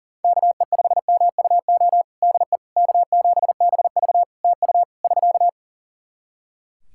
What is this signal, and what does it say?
Morse code, heard as a keyed tone, about 40 wpm, 700 Hz. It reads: KE5MUO DE K7BV TU <SK>
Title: K1USN SST QSO Practice 40wpm